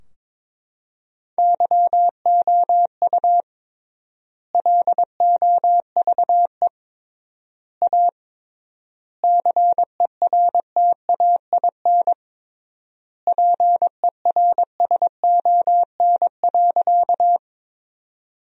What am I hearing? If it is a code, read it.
Morse code, heard as a keyed tone, about 22 wpm, 700 Hz. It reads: YOU LOVE A CERTAIN PERSON.
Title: 3x Word-Spacing - Sentences from Top 500 Words - 22wpm